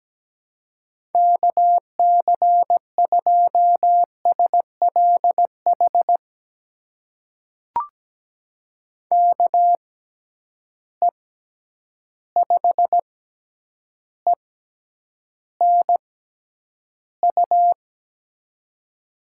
Morse code, heard as a keyed tone, about 17 wpm, spaced as for 9 wpm, 700 Hz. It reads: KC2SLH KE5ENU
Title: USA Call Signs - 6 Characters - 6x Character Spacing - 17wpm